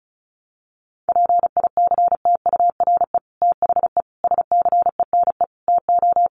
Morse code, heard as a keyed tone, about 34 wpm, 700 Hz. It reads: PICTURE THE SCENE TO
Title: Sentences from Top 1100 Words 35wpm